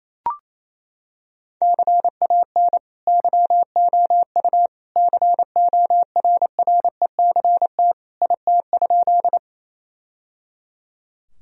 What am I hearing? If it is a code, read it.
Morse code, heard as a keyed tone, about 28 wpm, 700 Hz. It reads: CAN YOU CORRECT IT?
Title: Sentences from Top 1000 Words - Review Part 2 - 28wpm